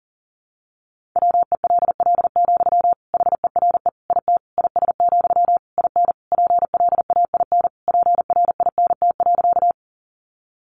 Morse code, heard as a keyed tone, about 40 wpm, 700 Hz. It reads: WELL, HERE IT IS, IN PLAIN PRINT.
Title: Sentences from Top 1400 Words 40wpm